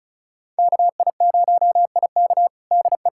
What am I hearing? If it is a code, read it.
Morse code, heard as a keyed tone, about 36 wpm, 700 Hz. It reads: KI0IK DE
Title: USA Callsign DE Callsign - 5 Characters - 35wpm